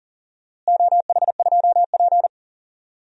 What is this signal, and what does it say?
Morse code, heard as a keyed tone, about 40 wpm, 700 Hz. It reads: OH2P
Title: DXCC-100 Call Signs - 4 Characters - Repeated 3x - 40wpm